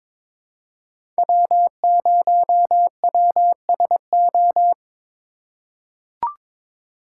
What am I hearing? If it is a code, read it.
Morse code, heard as a keyed tone, about 21 wpm, 700 Hz. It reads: W0WSO
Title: USA Call Signs - 5 Characters - 4x Character Spacing - 22wpm